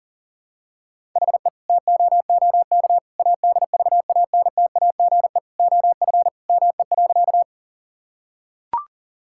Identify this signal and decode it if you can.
Morse code, heard as a keyed tone, about 40 wpm, 690 Hz. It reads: HE TOOK ADVANTAGE OF ME.
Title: Sentences from Top 1000 Words - Review Part 6 - 40wpm